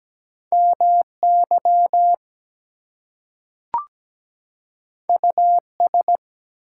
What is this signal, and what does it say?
Morse code, heard as a keyed tone, about 17 wpm, 700 Hz. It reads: MY US
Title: Two Letter Words - 17wpm